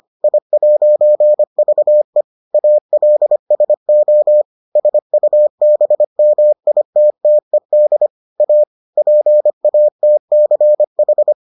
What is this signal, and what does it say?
Morse code, heard as a keyed tone, about 25 wpm, 600 Hz. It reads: I'VE ALSO SUBMITTED A PATCH